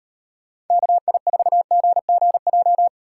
Morse code, heard as a keyed tone, about 38 wpm, 700 Hz. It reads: KI4GGJ